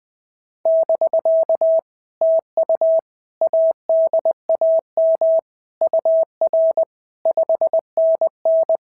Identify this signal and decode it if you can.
Morse code, heard as a keyed tone, about 20 wpm, 650 Hz. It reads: <BK> TU ADAM UR 5NN
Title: POTA QSO Practice - 20wpm